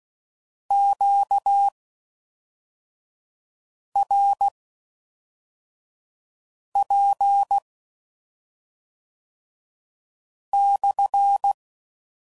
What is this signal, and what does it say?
Morse code, heard as a keyed tone, about 16 wpm, spaced as for 3 wpm, 800 Hz. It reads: QRP/